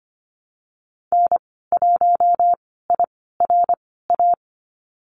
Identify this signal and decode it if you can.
Morse code, heard as a keyed tone, about 25 wpm, spaced as for 16 wpm, 690 Hz. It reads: N1IRA
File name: Lesson 15g - Number 5 - Call Signs 25wpm